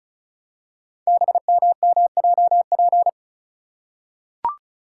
Morse code, heard as a keyed tone, about 36 wpm, 690 Hz. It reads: BMMJP
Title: Code Groups - 5 Letter:Numbers - 35wpm